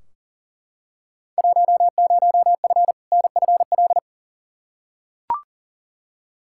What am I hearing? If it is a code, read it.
Morse code, heard as a keyed tone, about 40 wpm, 700 Hz. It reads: 10F NFL
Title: Field Day Exchanges Encore - 40wpm